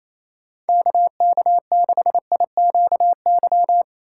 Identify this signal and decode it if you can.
Morse code, heard as a keyed tone, about 28 wpm, 700 Hz. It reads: KK6IQY
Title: USA Call Signs - 6 Characters - 2x Character Spacing - 28wpm